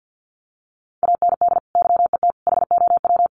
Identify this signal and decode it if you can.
Morse code, heard as a keyed tone, about 50 wpm, 700 Hz. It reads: AND YET HOW
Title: Sentences from Top 1300 Words 50wpm